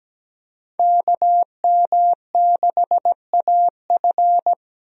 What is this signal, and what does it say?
Morse code, heard as a keyed tone, about 17 wpm, 700 Hz. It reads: KM6AF
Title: USA Call Signs - 5 Characters - 17wpm